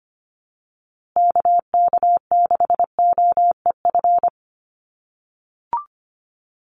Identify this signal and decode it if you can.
Morse code, heard as a keyed tone, about 25 wpm, 700 Hz. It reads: KK6OEF
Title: USA Call Signs Repeated 3x 25wpm